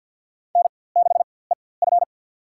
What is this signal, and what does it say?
Morse code, heard as a keyed tone, about 48 wpm, spaced as for 23 wpm, 700 Hz. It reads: N6EF
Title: USA Call Signs - 4x Character Spacing - 50wpm